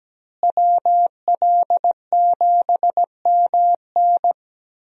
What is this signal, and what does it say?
Morse code, heard as a keyed tone, about 17 wpm, 700 Hz. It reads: WL7MN
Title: USA Call Signs - 5 Characters - Repeated 3x - 17wpm